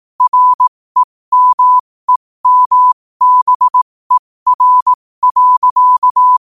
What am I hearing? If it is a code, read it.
Morse code, heard as a keyed tone, about 18 wpm, spaced as for 15 wpm, 1000 Hz. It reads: REMEMBER.
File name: qotdfiles / 15wpm